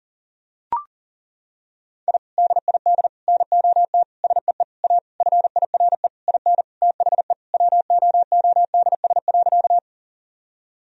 Morse code, heard as a keyed tone, about 37 wpm, 700 Hz. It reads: I DID NOT SEE A FIRE IN THE WOODS.
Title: Sentences from Top 1000 Words - Review Part 2 - 40wpm